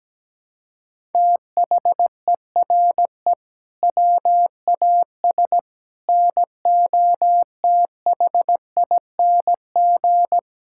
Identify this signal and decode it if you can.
Morse code, heard as a keyed tone, about 17 wpm, 700 Hz. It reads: THERE WAS NOTHING
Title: Sentences from Top 1000 Words - Review Part 5 - 17wpm